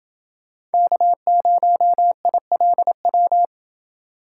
Morse code, heard as a keyed tone, about 30 wpm, 700 Hz. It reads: K0ILW